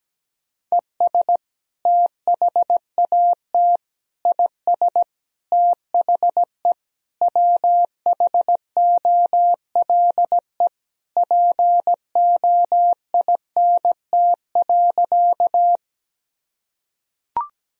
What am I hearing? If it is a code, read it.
Morse code, heard as a keyed tone, about 17 wpm, 700 Hz. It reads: ES THAT IS THE WHOLE POINT.
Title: Sentences from Top 400 Words and Abbreviations 17wpm